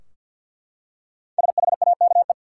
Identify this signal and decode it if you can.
Morse code, heard as a keyed tone, about 49 wpm, 700 Hz. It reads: SHAKE
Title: Five Letter Words - 50wpm